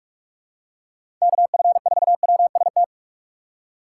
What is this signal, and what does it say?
Morse code, heard as a keyed tone, about 46 wpm, 700 Hz. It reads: KF4WST
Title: USA Call Signs - 6 Characters - 3x Character Spacing - 45wpm